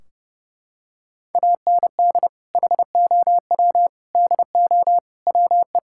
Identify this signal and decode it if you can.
Morse code, heard as a keyed tone, about 30 wpm, 700 Hz. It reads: AND HOW DO WE
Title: Sentences from Top 1600 Words 30wpm